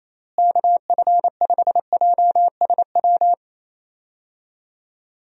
Morse code, heard as a keyed tone, about 27 wpm, 700 Hz. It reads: KF5JSW
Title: USA Call Signs - 6 Characters - Repeated 3x - 28wpm